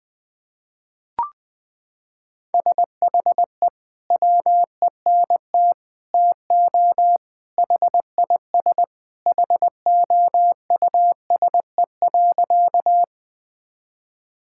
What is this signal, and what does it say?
Morse code, heard as a keyed tone, about 20 wpm, 700 Hz. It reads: SHE WENT TO HIS HOUSE.
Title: Sentences with 2-Word Phrases - 20wpm